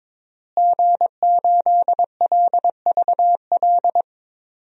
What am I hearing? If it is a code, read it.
Morse code, heard as a keyed tone, about 22 wpm, 700 Hz. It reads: G8LVL
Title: DXCC-100 Call Signs - 5 Characters - Repeated 3x - 22wpm